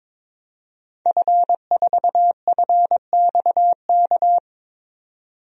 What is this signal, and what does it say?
Morse code, heard as a keyed tone, about 21 wpm, 700 Hz. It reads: F4FXK
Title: DXCC-100 Call Signs - 5 Characters - 22wpm